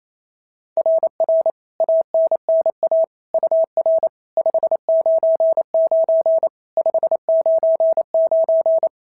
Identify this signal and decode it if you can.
Morse code, heard as a keyed tone, about 25 wpm, 650 Hz. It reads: RR ANNA UR 599 599
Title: SOTA QSO Practice 28wpm